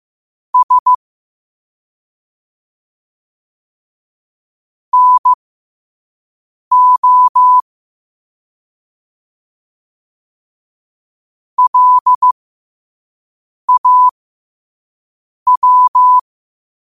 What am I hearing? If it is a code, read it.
Morse code, heard as a keyed tone, about 15 wpm, spaced as for 5 wpm, 1000 Hz. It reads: S NO LAW
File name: qotdfiles / 5wpm